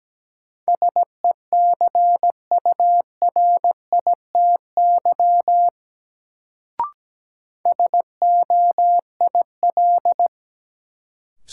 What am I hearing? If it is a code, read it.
Morse code, heard as a keyed tone, about 17 wpm, 700 Hz. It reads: SECURITY SOIL
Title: Sentences from Top 1000 Words - Review Part 4 - New Word Drill - 17wpm